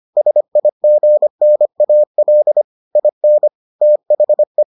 Morse code, heard as a keyed tone, about 25 wpm, 600 Hz. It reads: SIGNAL IN THE